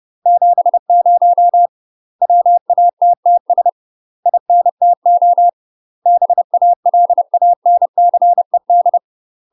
Morse code, heard as a keyed tone, about 30 wpm, 700 Hz. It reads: 70 WATTS INTO BALANCED
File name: UK MASTER QSOs TRAINING / 30WPM / UK MASTER QSOs 0003